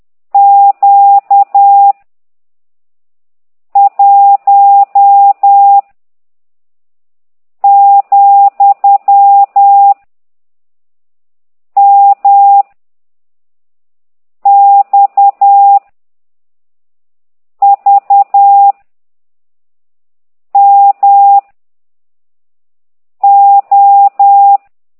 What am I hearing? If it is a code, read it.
Morse code, heard as a keyed tone, about 10 wpm, 800 Hz. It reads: Q 1 , M X V M O
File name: ct2idw / exerc15